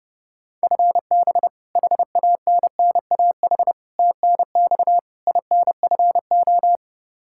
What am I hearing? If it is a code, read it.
Morse code, heard as a keyed tone, about 30 wpm, 700 Hz. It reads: FB HANNAH TNX INFO
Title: Practice QSO - Exchange 3 - More Info - 30wpm